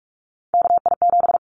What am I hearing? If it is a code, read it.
Morse code, heard as a keyed tone, about 46 wpm, 700 Hz. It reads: KI7
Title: Callsign Prefixes for Top 25 DXCC - 45wpm